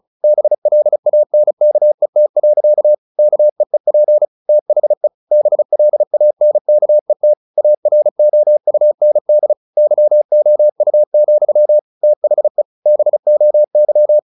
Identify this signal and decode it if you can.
Morse code, heard as a keyed tone, about 34 wpm, 600 Hz. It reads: BLANKET. KEEP THE BLANKET AROUND YOU, THE BOY
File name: the old man and the sea / the old man and the sea 1 35wpm